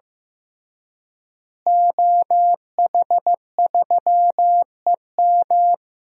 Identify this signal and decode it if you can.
Morse code, heard as a keyed tone, about 15 wpm, 700 Hz. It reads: OH3EM
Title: DXCC-100 Call Signs - 5 Characters - Repeated 3x - 15wpm